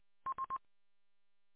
There is no code.